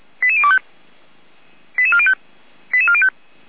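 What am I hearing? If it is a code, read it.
Five-tone selective call (ZVEI), heard as quick runs of five tones, tones of about 70 ms. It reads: 88025 88405 88405
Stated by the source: ZVEI five-tone call received on SDR, feuerwehr-orlamuende.org fire brigade.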